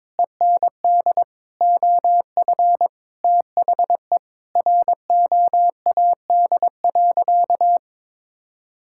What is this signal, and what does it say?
Morse code, heard as a keyed tone, about 22 wpm, 700 Hz. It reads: END OF THE ROAD.
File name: practice / 22wpm / Sentences with 4-Word Phrases - 22wpm